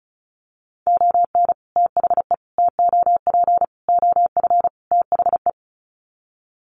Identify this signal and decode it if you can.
Morse code, heard as a keyed tone, about 34 wpm, 700 Hz. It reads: ON THE TOP OF THE
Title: Sets of 5 Words Encore 35wpm